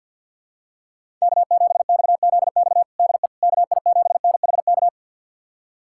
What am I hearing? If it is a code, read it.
Morse code, heard as a keyed tone, about 52 wpm, 700 Hz. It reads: K7XZX DE KI7NHK
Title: USA Callsign DE Callsign 50wpm